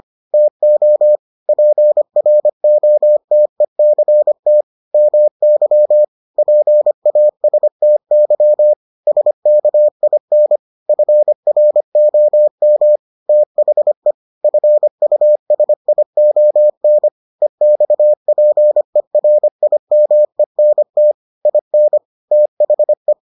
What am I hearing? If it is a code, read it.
Morse code, heard as a keyed tone, about 25 wpm, 600 Hz. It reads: TO PROTECT MY PASTY SKIN FROM THE FUSION EXPERIMENT IN THE